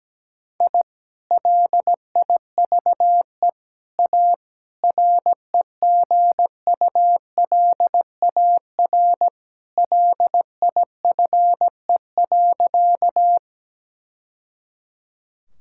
Morse code, heard as a keyed tone, about 17 wpm, 700 Hz. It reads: I LIVE A REGULAR LIFE.
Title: Sentences from Top 1000 Words - Review Part 6 - 17wpm